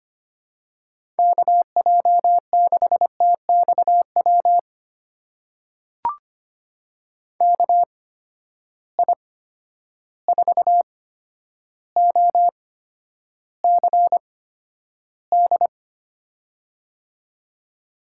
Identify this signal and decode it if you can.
Morse code, heard as a keyed tone, about 25 wpm, spaced as for 11 wpm, 700 Hz. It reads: KJ6TXW KI4OCD
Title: USA Call Signs - 6 Characters - 8x Character Spacing - 25wpm